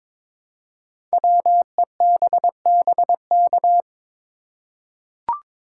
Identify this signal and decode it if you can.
Morse code, heard as a keyed tone, about 22 wpm, 700 Hz. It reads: WEBBK
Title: Code Groups - 5 Letter:Numbers - 22wpm